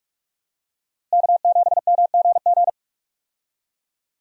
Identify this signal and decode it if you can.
Morse code, heard as a keyed tone, about 46 wpm, 700 Hz. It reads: K7MGG
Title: USA Call Signs - 5 Characters - Repeated 3x - 45wpm